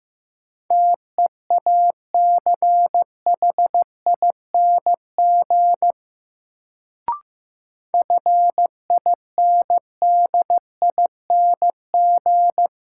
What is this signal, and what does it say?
Morse code, heard as a keyed tone, about 15 wpm, 700 Hz. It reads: TEACHING FINDING
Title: Sentences from Top 1100 Words - New Word Drill - 15wpm